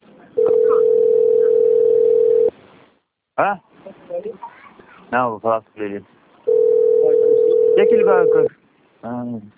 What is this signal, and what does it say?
Ringback tone (North America), 440 + 480 Hz, with speech.
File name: out- - - - - .65344